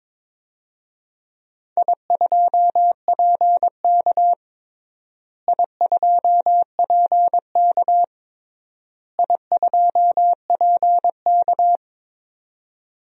Morse code, heard as a keyed tone, about 21 wpm, 700 Hz. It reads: I2PK I2PK I2PK
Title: DXCC-100 Call Signs - 4 Characters - Repeated 3x - 22wpm